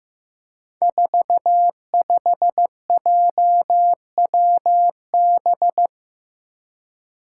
Morse code, heard as a keyed tone, about 15 wpm, 700 Hz. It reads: 45JWB
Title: Five Challenging Letter-Number Combinations 15wpm